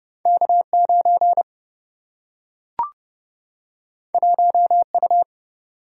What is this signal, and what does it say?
Morse code, heard as a keyed tone, about 29 wpm, 700 Hz. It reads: K9 1U
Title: Single-Letter Single-Number 30wpm